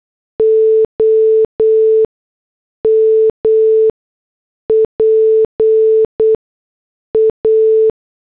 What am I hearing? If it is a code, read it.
Morse code, heard as a keyed tone, about 8 wpm, spaced as for 6 wpm, 440 Hz. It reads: OMPA